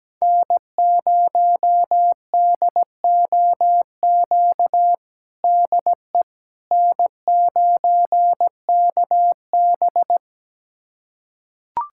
Morse code, heard as a keyed tone, about 17 wpm, 700 Hz. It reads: N0DOQ DE N9KB
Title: USA Callsign DE Callsign 17wpm